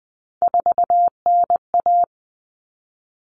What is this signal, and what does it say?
Morse code, heard as a keyed tone, about 20 wpm, 700 Hz. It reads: 4NA
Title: Common Cut Numbers - 3-Digit 20wpm